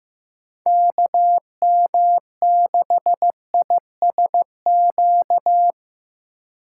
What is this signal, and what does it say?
Morse code, heard as a keyed tone, about 17 wpm, 700 Hz. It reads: KM6ISQ
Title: Speed Racing - USA 2x3 Call Signs 15wpm